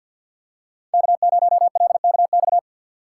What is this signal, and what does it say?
Morse code, heard as a keyed tone, about 45 wpm, 700 Hz. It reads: K0LKX
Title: USA Call Signs - 5 Characters - 2x Character Spacing - 50wpm